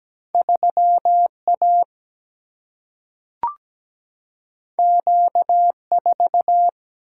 Morse code, heard as a keyed tone, about 17 wpm, 700 Hz. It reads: 3 A Q 4
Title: Single-Letter Single-Number 17wpm